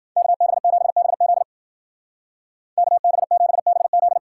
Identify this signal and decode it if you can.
Morse code, heard as a keyed tone, about 49 wpm, 700 Hz. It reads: KB7BZ KB7BZ
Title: Speed Racing - USA 2x2 Call Signs 40wpm